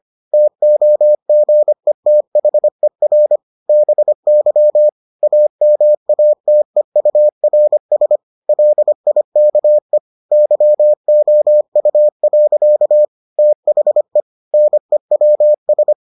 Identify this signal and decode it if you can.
Morse code, heard as a keyed tone, about 25 wpm, 600 Hz. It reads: TOGETHER BY AMATEURS LIKE YOU. THE NEWS